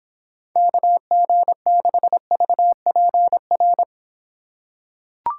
Morse code, heard as a keyed tone, about 24 wpm, 700 Hz. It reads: KG6VPR